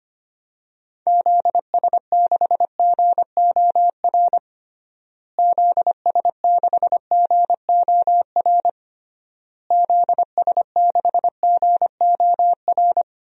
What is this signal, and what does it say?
Morse code, heard as a keyed tone, about 25 wpm, 700 Hz. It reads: ZS6GOR ZS6GOR ZS6GOR
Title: DXCC-100 Call Signs - 6 Characters - Repeated 3x - 25wpm